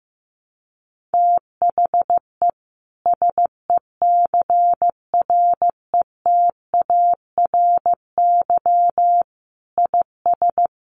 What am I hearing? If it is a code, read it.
Morse code, heard as a keyed tone, about 14 wpm, 700 Hz. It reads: THE SECRETARY IS
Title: Sentences from Top 1300 Words 15wpm